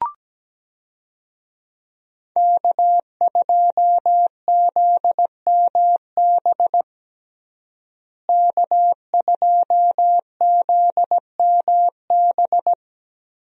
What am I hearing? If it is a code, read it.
Morse code, heard as a keyed tone, about 17 wpm, 700 Hz. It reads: K2ZMB K2ZMB
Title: USA Call Signs - 5 Characters - Repeated 3x - 17wpm